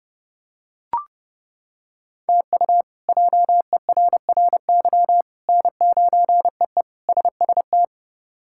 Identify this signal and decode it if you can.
Morse code, heard as a keyed tone, about 32 wpm, 700 Hz. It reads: TU JERRY N9EE SST
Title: K1USN SST QSO Practice - 30wpm